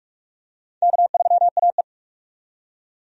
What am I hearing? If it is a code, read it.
Morse code, heard as a keyed tone, about 47 wpm, 700 Hz. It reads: K3AE
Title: USA Call Signs - 4 Characters - 45wpm